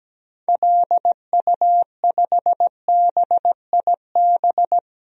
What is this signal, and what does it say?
Morse code, heard as a keyed tone, about 17 wpm, 700 Hz. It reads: LU5BIB
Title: DXCC-100 Call Signs - 6 Characters - 17wpm